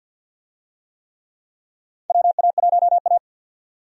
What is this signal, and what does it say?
Morse code, heard as a keyed tone, about 51 wpm, 700 Hz. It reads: WA1A
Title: USA Call Signs - 4 Characters - 50wpm